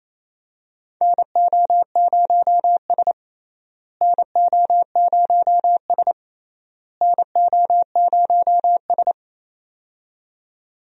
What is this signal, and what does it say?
Morse code, heard as a keyed tone, about 28 wpm, 700 Hz. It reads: NO0S NO0S NO0S
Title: USA Call Signs - 4 Characters - Repeated 3x - 28wpm